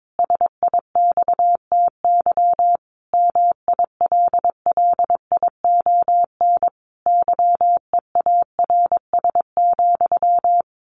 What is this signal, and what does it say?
Morse code, heard as a keyed tone, about 22 wpm, 700 Hz. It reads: SIXTY MILLION YEARS,
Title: A Sound Of Thunder 22wpm